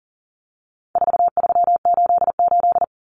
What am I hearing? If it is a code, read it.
Morse code, heard as a keyed tone, about 40 wpm, 700 Hz. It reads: 4388
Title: Common Cut Numbers - 4-Digit 40wpm